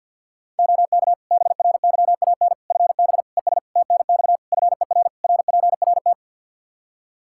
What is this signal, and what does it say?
Morse code, heard as a keyed tone, about 48 wpm, 700 Hz. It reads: OK BRYAN FB ES TNX FER RPRT
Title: Practice QSO - Exchange 2 - Info - 50wpm